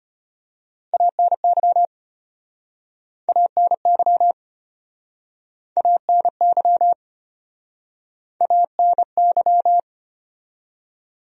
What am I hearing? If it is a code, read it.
Morse code, heard as a keyed tone, about 33 wpm, 700 Hz. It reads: ANY ANY ANY ANY